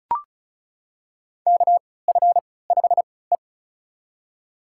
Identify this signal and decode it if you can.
Morse code, heard as a keyed tone, about 36 wpm, spaced as for 20 wpm, 700 Hz. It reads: KF5E
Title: USA Call Signs - 4 Characters - 3x Character Spacing - 35wpm